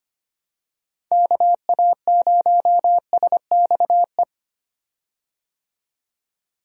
Morse code, heard as a keyed tone, about 25 wpm, 700 Hz. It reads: KA0SXE
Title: USA Call Signs - 6 Characters - 25wpm